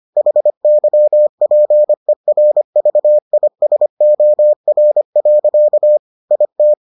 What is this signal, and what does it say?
Morse code, heard as a keyed tone, about 25 wpm, 600 Hz. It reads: HYPERVISOR. IT